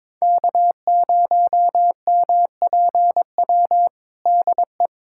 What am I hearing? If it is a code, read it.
Morse code, heard as a keyed tone, about 22 wpm, 700 Hz. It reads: K0MPW DE
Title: USA Callsign DE Callsign 22wpm